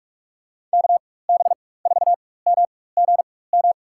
Morse code, heard as a keyed tone, about 47 wpm, spaced as for 21 wpm, 700 Hz. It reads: KB4MGM
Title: USA Call Signs - 6 Characters - 4x Character Spacing - 45wpm